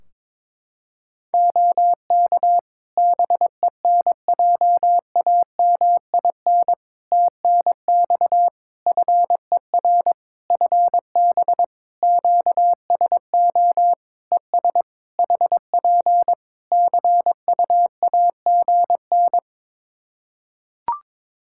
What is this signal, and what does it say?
Morse code, heard as a keyed tone, about 22 wpm, 700 Hz. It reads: OK BENJAMIN TNX FER FB QSO ES HP CUAGN
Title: Practice QSO - Exchange 4 - Ending - 22wpm